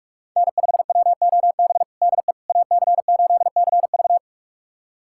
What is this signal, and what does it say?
Morse code, heard as a keyed tone, about 46 wpm, 700 Hz. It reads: N5WOB DE AC8CV
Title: USA Callsign DE Callsign - 5 Characters - 45wpm